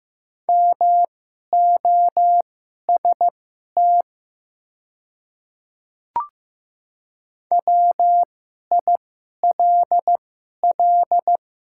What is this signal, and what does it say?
Morse code, heard as a keyed tone, about 15 wpm, spaced as for 11 wpm, 700 Hz. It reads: MOST WILL
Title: 2x Character Spacing - Top 100 Words - 15wpm